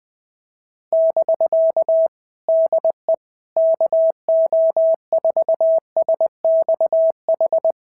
Morse code, heard as a keyed tone, about 20 wpm, 650 Hz. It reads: <BK> DE KO4SXH
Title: POTA QSO Practice - 20wpm